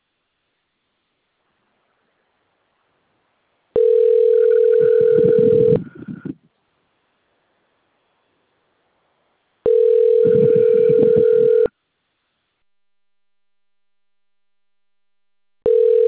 Ringback tone (North America), 440 + 480 Hz.